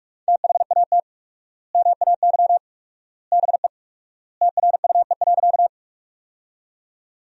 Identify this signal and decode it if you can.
Morse code, heard as a keyed tone, about 46 wpm, 700 Hz. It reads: THAT MAY BE TRUE.
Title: 4x Word-Spacing - Sentences from Top 400 Words - 45wpm